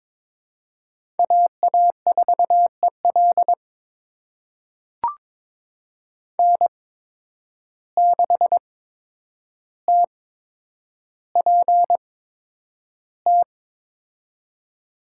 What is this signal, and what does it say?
Morse code, heard as a keyed tone, about 21 wpm, spaced as for 9 wpm, 700 Hz. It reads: AA4EL N6TPT
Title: USA Call Signs - 5 Characters - 8x Character Spacing - 22wpm